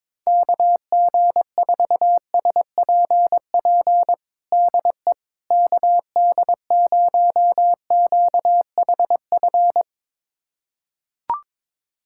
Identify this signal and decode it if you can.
Morse code, heard as a keyed tone, about 22 wpm, 700 Hz. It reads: KG4SPP DE KD0QHF
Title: USA Callsign DE Callsign - 6 Characters - 22wpm